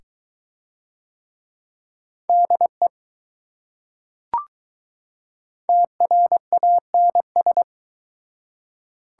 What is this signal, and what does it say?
Morse code, heard as a keyed tone, about 20 wpm, 700 Hz. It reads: DE TRANS